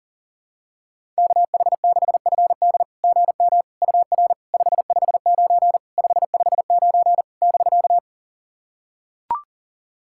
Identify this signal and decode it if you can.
Morse code, heard as a keyed tone, about 40 wpm, 690 Hz. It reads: KH6FD GM UR 559 559 <BK>
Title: SOTA QSO Practice 40wpm